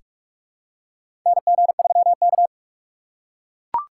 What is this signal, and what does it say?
Morse code, heard as a keyed tone, about 44 wpm, 700 Hz. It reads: NG3K
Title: USA Call Signs - 4 Characters - 4x Character Spacing - 45wpm